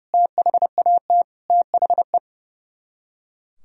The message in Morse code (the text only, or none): THAT THE